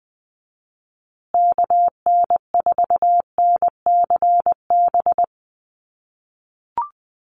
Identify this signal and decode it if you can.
Morse code, heard as a keyed tone, about 20 wpm, 700 Hz. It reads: KN4NCB
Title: USA Call Signs Repeated 3x 20wpm